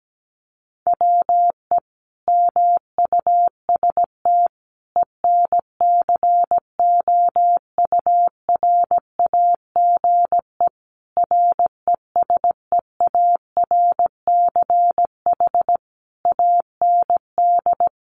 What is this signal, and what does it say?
Morse code, heard as a keyed tone, about 17 wpm, 700 Hz. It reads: WE MUST ENCOURAGE RESEARCH AND
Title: Sentences from Top 1400 Words 17wpm